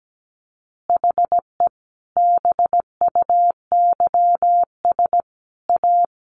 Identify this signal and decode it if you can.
Morse code, heard as a keyed tone, about 17 wpm, 700 Hz. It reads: HE BUYS A
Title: Easy Phrases 17wpm